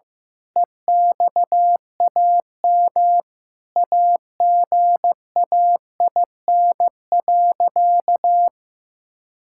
Morse code, heard as a keyed tone, about 16 wpm, 700 Hz. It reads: EXAM AGAIN.
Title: Sentences from Top 1000 Words - Review Part 5 - 15wpm